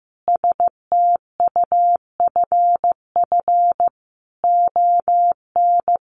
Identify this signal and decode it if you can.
Morse code, heard as a keyed tone, about 15 wpm, 700 Hz. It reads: STUFF ON
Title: Sentences from Top 1400 Words 15wpm